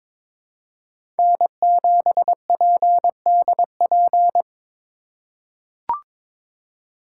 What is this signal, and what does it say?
Morse code, heard as a keyed tone, about 22 wpm, 700 Hz. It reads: N7PDP
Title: USA Call Signs - 5 Characters - 4x Character Spacing - 22wpm